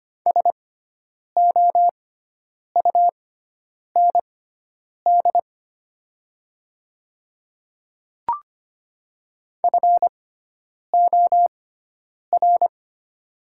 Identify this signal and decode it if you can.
Morse code, heard as a keyed tone, about 25 wpm, spaced as for 9 wpm, 700 Hz. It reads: SOUND FOR
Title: 6x Character Spacing - Top 100 Words - 25wpm